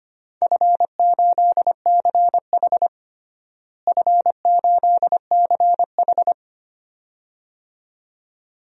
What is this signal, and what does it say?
Morse code, heard as a keyed tone, about 25 wpm, 700 Hz. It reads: F8CH F8CH
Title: DXCC-100 Call Signs - 4 Characters - Repeated 2x - 25wpm